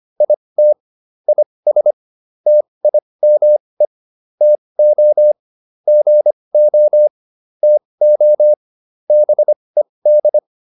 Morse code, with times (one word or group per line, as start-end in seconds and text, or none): 0.2-0.7 IT
1.3-1.9 IS
2.5-3.9 TIME
4.4-5.3 TO
5.9-7.1 GO
7.6-8.5 TO
9.1-10.4 BED